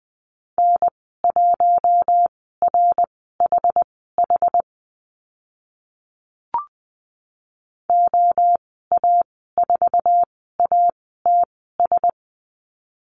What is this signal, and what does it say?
Morse code, heard as a keyed tone, about 20 wpm, spaced as for 14 wpm, 700 Hz. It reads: N1RHH OA4ATS
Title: 082 Lesson 14g - Number 2 - Call Signs 20wpm